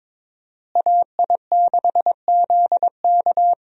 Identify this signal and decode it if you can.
Morse code, heard as a keyed tone, about 22 wpm, 700 Hz. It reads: AI6ZK
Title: USA Call Signs - 5 Characters - Repeated 3x - 22wpm